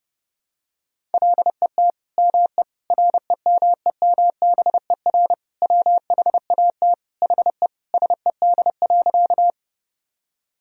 Morse code, heard as a keyed tone, about 32 wpm, 700 Hz. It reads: LET ME REMEMBER WHAT HE SED.
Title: Sentences from Top 400 Words and Abbreviations 30wpm